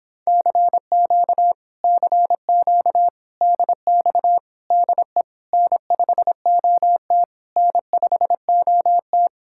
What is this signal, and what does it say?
Morse code, heard as a keyed tone, about 31 wpm, 700 Hz. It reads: CQ CQ DX DE N5OT N5OT